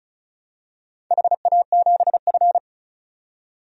Morse code, heard as a keyed tone, about 36 wpm, 700 Hz. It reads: HA7F
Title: DXCC-100 Call Signs - 4 Characters - 35wpm